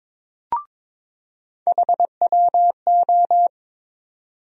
Morse code, heard as a keyed tone, about 22 wpm, 700 Hz. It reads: HWO
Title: USA Call Sign Postfixes - 3 Letters 22wpm